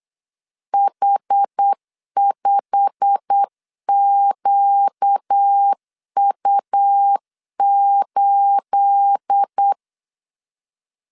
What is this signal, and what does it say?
Morse code, heard as a keyed tone, about 8 wpm, 800 Hz. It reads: H5QU8